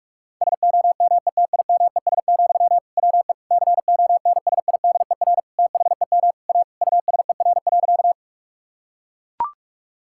Morse code, heard as a keyed tone, about 45 wpm, 700 Hz. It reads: SOMETIMES, WE CONSIDER THEM A USER.
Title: Sentences from Top 1000 Words - Review Part 5 - 45wpm